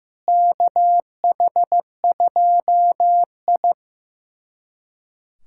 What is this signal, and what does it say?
Morse code, heard as a keyed tone, about 16 wpm, 700 Hz. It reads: KH2I
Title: USA Callsign DE Callsign - 4 Characters - 15wpm